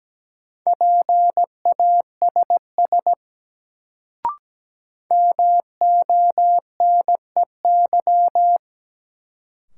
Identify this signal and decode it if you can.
Morse code, heard as a keyed tone, about 17 wpm, 700 Hz. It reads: PASS MONEY
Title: Sentences from Top 400 Words - New Word Drill - 17wpm